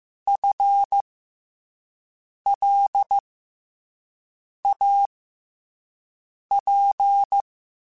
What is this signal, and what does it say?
Morse code, heard as a keyed tone, about 15 wpm, spaced as for 5 wpm, 800 Hz. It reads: FLAP
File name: FL Runs / FL Runs mp3s / FL Runs @5